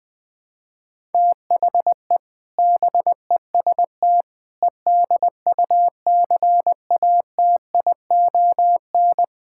Morse code, heard as a keyed tone, about 21 wpm, 700 Hz. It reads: THE BEST EDUCATION
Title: Sentences from Top 1000 Words - Review Part 6 - 20wpm